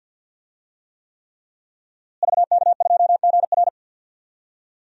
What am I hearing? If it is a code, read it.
Morse code, heard as a keyed tone, about 49 wpm, 700 Hz. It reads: VK2GR
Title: DXCC-100 Call Signs - 5 Characters - Repeated 3x - 50wpm